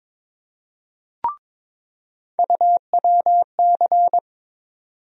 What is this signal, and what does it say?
Morse code, heard as a keyed tone, about 22 wpm, 700 Hz. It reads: UWC
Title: USA Call Sign Postfixes - 3 Letters 22wpm